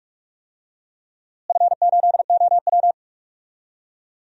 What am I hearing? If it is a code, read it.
Morse code, heard as a keyed tone, about 46 wpm, 700 Hz. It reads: F8OW
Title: DXCC-100 Call Signs - 4 Characters - Repeated 3x - 45wpm